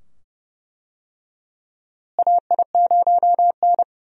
Morse code, heard as a keyed tone, about 31 wpm, 700 Hz. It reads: AI0N
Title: USA Call Signs - 4 Characters - Repeated 2x - 30wpm